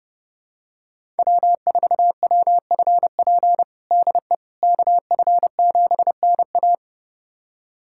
Morse code, heard as a keyed tone, about 30 wpm, 700 Hz. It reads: W4WFP DE KF7NA
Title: USA Callsign DE Callsign - 5 Characters - 30wpm